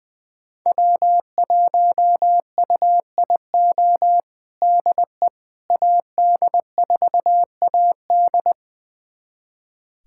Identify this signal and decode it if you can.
Morse code, heard as a keyed tone, about 20 wpm, 700 Hz. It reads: W1UIO DE AD4AD
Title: USA Callsign DE Callsign - 5 Characters - 20wpm